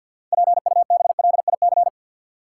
Morse code, heard as a keyed tone, about 52 wpm, 700 Hz. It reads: PUBLIC
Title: Six Letter Words - 50wpm